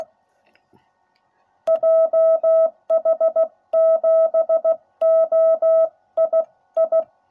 Morse code, heard as a keyed tone, about 16 wpm, spaced as for 14 wpm, 660 Hz. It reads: JH7OII